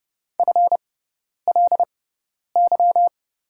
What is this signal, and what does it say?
Morse code, heard as a keyed tone, about 30 wpm, spaced as for 10 wpm, 700 Hz. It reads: FLY
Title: 6x Character Spacing - Top 500 Words - 30wpm